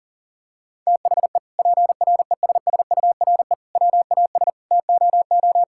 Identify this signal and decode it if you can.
Morse code, heard as a keyed tone, about 40 wpm, 700 Hz. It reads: THE PRESSURE WAS TOO
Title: Sentences from Top 1000 Words - Review Part 6 - 40wpm